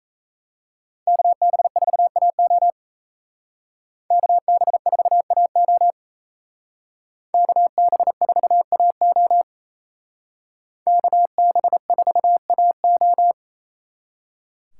Morse code, heard as a keyed tone, about 32 wpm, 700 Hz. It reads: KB4AO KB4AO KB4AO KB4AO
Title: Speed Racing - USA Call Signs 28wpm